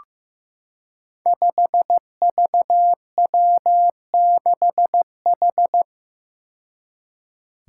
Morse code, heard as a keyed tone, about 15 wpm, 700 Hz. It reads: 5VW6H
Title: Five Challenging Letter-Number Combinations 15wpm